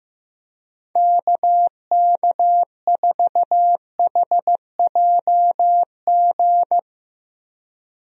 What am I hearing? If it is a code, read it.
Morse code, heard as a keyed tone, about 15 wpm, 700 Hz. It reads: KK4HJG
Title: USA Call Signs - 6 Characters - Repeated 3x - 15wpm